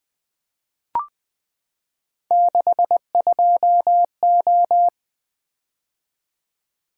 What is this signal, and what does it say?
Morse code, heard as a keyed tone, about 20 wpm, 700 Hz. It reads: 62O
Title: Three Letter-Number Combinations - 20wpm